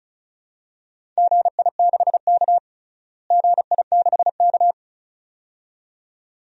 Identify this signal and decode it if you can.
Morse code, heard as a keyed tone, about 36 wpm, 700 Hz. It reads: GI6K GI6K
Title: DXCC-100 Call Signs - 4 Characters - Repeated 2x - 35wpm